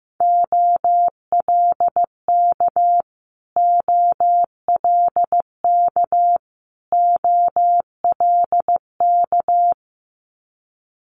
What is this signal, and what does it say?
Morse code, heard as a keyed tone, about 15 wpm, 700 Hz. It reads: OLK OLK OLK
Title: Multi-letter Phonemes Repeated 3x 15wpm